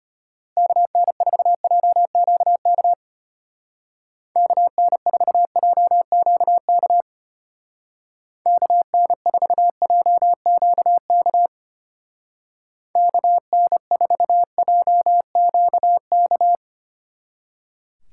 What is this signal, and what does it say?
Morse code, heard as a keyed tone, about 30 wpm, 700 Hz. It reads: KN4JQK KN4JQK KN4JQK KN4JQK
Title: Speed Racing - USA 2x3 Call Signs 25wpm